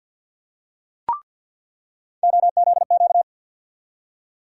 Morse code, heard as a keyed tone, about 46 wpm, 700 Hz. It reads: OZQ
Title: USA Call Sign Postfixes - 3 Letters 50wpm